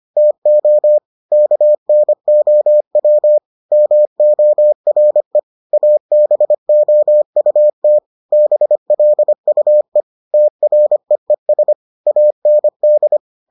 Morse code, heard as a keyed tone, about 25 wpm, 600 Hz. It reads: TO KNOW MORE ABOUT BLUE TREES AND